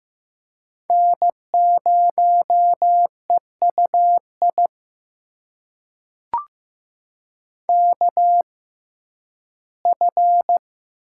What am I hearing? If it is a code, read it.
Morse code, heard as a keyed tone, about 15 wpm, spaced as for 11 wpm, 700 Hz. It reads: N0EUI KF
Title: USA Call Signs - 5 Characters - 6x Character Spacing - 15wpm